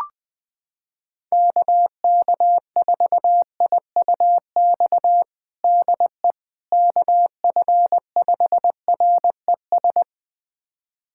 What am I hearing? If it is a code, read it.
Morse code, heard as a keyed tone, about 21 wpm, 700 Hz. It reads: KK4IUX DE KF5RES
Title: USA Callsign DE Callsign 20wpm